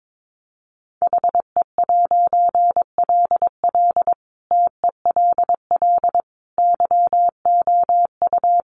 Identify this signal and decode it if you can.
Morse code, heard as a keyed tone, about 22 wpm, 700 Hz. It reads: HE'LL TELL YOU
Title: A Sound Of Thunder 22wpm